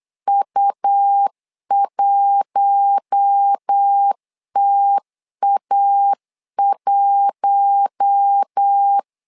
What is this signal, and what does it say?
Morse code, heard as a keyed tone, about 8 wpm, 800 Hz. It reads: U1TA1
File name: LEZIONI / 4 Q1 8wpm 3035